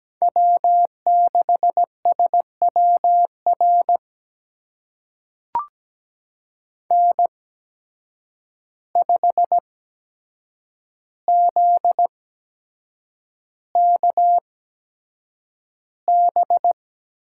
Morse code, heard as a keyed tone, about 17 wpm, spaced as for 7 wpm, 700 Hz. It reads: W6SWR N5ZKB
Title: USA Call Signs - 5 Characters - 8x Character Spacing - 17wpm